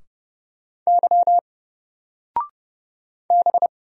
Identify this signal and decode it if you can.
Morse code, heard as a keyed tone, about 31 wpm, 700 Hz. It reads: Y B
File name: Single-Letter-Number-30wpm